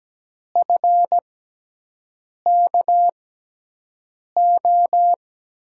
Morse code, heard as a keyed tone, about 17 wpm, spaced as for 6 wpm, 700 Hz. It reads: FKO
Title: USA Call Signs - 6 Characters - 6x Character Spacing - 17wpm